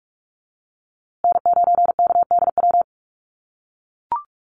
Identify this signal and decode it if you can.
Morse code, heard as a keyed tone, about 44 wpm, 700 Hz. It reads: N9KDW
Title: USA Call Signs Repeated 3x 45wpm